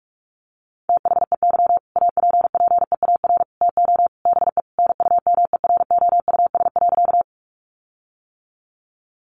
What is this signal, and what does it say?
Morse code, heard as a keyed tone, about 44 wpm, 700 Hz. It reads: THEY APPEAR TO BE NUMEROUS.
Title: Sentences from Top 1400 Words 45wpm